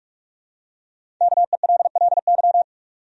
Morse code, heard as a keyed tone, about 46 wpm, 700 Hz. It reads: KELLY
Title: USA Common Names - Max 5 Letters - 45wpm